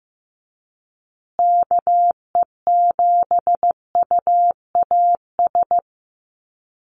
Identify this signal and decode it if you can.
Morse code, heard as a keyed tone, about 14 wpm, 700 Hz. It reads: KE7UAS
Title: USA Call Signs Repeated 3x 15wpm